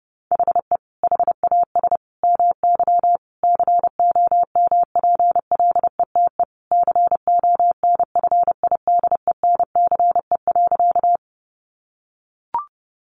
Morse code, heard as a keyed tone, about 30 wpm, 700 Hz. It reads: HE HAS MY COMPLETE CONFIDENCE.
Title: Sentences from Top 1500 Words 30wpm